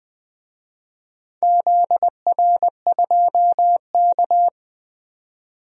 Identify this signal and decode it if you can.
Morse code, heard as a keyed tone, about 20 wpm, 700 Hz. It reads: ZR2K
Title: DXCC-100 Call Signs - 4 Characters - 20wpm